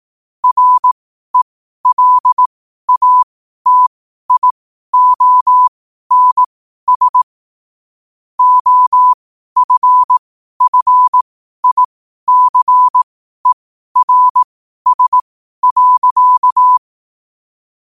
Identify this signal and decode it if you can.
Morse code, heard as a keyed tone, about 18 wpm, spaced as for 12 wpm, 1000 Hz. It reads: RELATIONS OFFICERS.
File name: qotdfiles / 12wpm